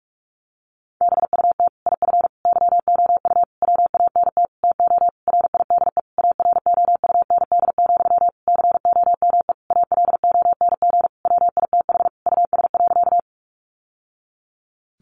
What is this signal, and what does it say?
Morse code, heard as a keyed tone, about 44 wpm, 700 Hz. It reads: BUT IF YOU WANT TO RIDE AROUND, COME ALONG WITH US.
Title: Sentences from Top 1200 Words 45wpm